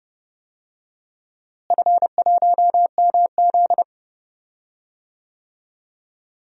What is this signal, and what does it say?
Morse code, heard as a keyed tone, about 31 wpm, 700 Hz. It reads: F1MZ
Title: DXCC-100 Call Signs - 4 Characters - Repeated 3x - 30wpm